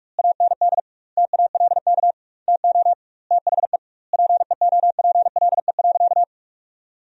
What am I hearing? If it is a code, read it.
Morse code, heard as a keyed tone, about 42 wpm, 700 Hz. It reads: AND TALK TO THE PEOPLE.
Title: 2x Word-Spacing - Sentences from Top 400 Words - 45wpm